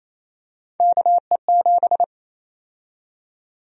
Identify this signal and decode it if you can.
Morse code, heard as a keyed tone, about 27 wpm, 700 Hz. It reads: KE7
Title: USA Call Sign Prefixes - 28wpm